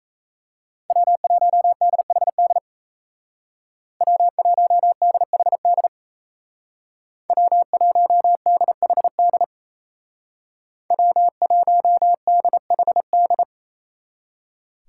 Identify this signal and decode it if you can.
Morse code, heard as a keyed tone, about 34 wpm, 700 Hz. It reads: W1DHD W1DHD W1DHD W1DHD
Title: Speed Racing - USA 1x3 Call Signs 28wpm